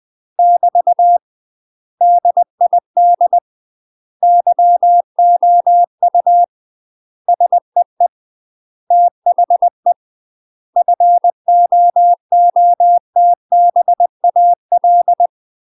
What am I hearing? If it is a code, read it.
Morse code, heard as a keyed tone, about 20 wpm, 700 Hz. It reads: <BT> DID YOU SEE THE FOOTBAL
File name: UK MASTER QSOs TRAINING / 20WPM / UK MASTER QSOs 0058